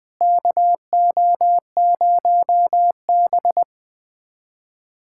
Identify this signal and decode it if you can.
Morse code, heard as a keyed tone, about 21 wpm, 700 Hz. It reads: KO0B
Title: USA Callsign DE Callsign - 4 Characters - 20wpm